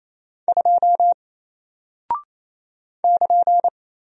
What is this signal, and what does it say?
Morse code, heard as a keyed tone, about 28 wpm, 700 Hz. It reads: 2 <KN>
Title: Single Character (Letter:Number:Punctuation:Prosign) Alt 28wpm